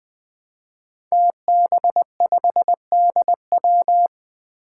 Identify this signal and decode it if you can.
Morse code, heard as a keyed tone, about 20 wpm, 700 Hz. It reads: TB5DW
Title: Five Challenging Letter-Number Combinations 20wpm